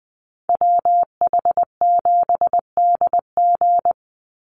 Morse code, heard as a keyed tone, about 20 wpm, 700 Hz. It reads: WH7DG
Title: USA Call Signs 20wpm